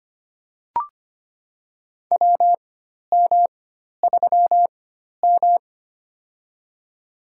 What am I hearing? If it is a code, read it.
Morse code, heard as a keyed tone, about 25 wpm, spaced as for 12 wpm, 700 Hz. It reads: WM3M
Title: USA Call Signs - 4x Character Spacing - 25wpm